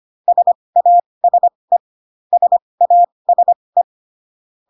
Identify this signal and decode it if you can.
Morse code, heard as a keyed tone, about 25 wpm, spaced as for 20 wpm, 700 Hz. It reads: SASE SASE